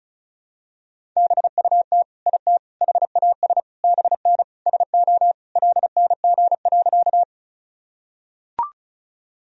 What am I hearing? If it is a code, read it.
Morse code, heard as a keyed tone, about 36 wpm, 700 Hz. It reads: BUT IT HAS BN SO LNG.
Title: Sentences from Top 100 Words and Abbreviations 35wpm